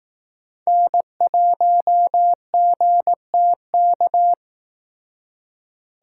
Morse code, heard as a keyed tone, about 18 wpm, 700 Hz. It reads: N1GTK
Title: Speed Racing - USA 1x3 Call Signs 15wpm